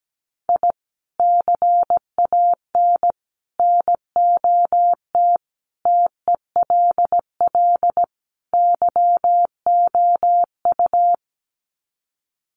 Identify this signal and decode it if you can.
Morse code, heard as a keyed tone, about 17 wpm, 700 Hz. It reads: I CAN NOT TELL YOU
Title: Sets of 5 Words 17wpm